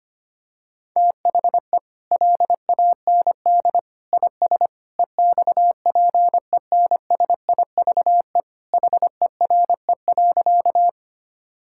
Morse code, heard as a keyed tone, about 25 wpm, 700 Hz. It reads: THE LAND IS EXPENSIVE HERE.
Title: Sentences from Top 1000 Words - Review Part 6 - 25wpm